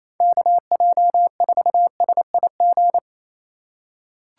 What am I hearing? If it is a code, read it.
Morse code, heard as a keyed tone, about 28 wpm, 700 Hz. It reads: KJ4SIG
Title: USA Call Signs - 6 Characters - 2x Character Spacing - 28wpm